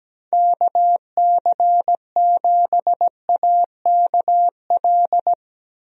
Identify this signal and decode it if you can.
Morse code, heard as a keyed tone, about 17 wpm, 700 Hz. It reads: KC7AKL
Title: USA Call Signs - 6 Characters - Repeated 2x - 17wpm